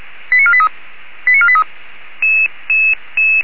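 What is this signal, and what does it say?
Five-tone selective call (ZVEI), heard as quick runs of five tones, tones of about 70 ms. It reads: 89483 89483
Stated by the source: Received by SDR on a ZVEI selective-call channel.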